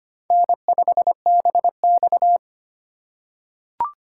Morse code, heard as a keyed tone, about 25 wpm, 700 Hz. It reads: N5BX
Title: USA Call Signs - 4 Characters - 4x Character Spacing - 25wpm